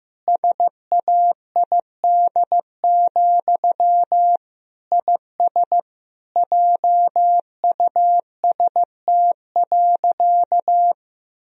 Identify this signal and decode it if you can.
Morse code, heard as a keyed tone, about 15 wpm, 700 Hz. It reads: SAID, IS JUST.
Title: Sentences from Top 1000 Words - Review Part 4 - 15wpm